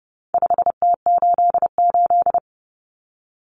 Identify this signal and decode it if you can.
Morse code, heard as a keyed tone, about 30 wpm, 700 Hz. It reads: 5T88
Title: Common Cut Numbers - 4-Digit 30wpm